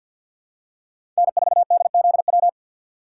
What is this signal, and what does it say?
Morse code, heard as a keyed tone, about 53 wpm, 700 Hz. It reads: N4DZW
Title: USA Call Signs - 5 Characters - 6x Character Spacing - 50wpm